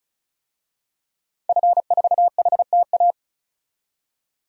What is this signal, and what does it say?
Morse code, heard as a keyed tone, about 34 wpm, 700 Hz. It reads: F4HTA
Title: DXCC-100 Call Signs - 5 Characters - 35wpm